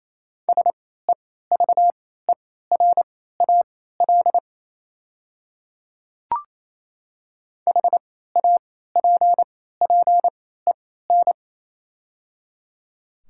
Morse code, heard as a keyed tone, about 28 wpm, spaced as for 16 wpm, 700 Hz. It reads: SEVERAL HAPPEN
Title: 3x Character Spacing - Top 500 Words - 28wpm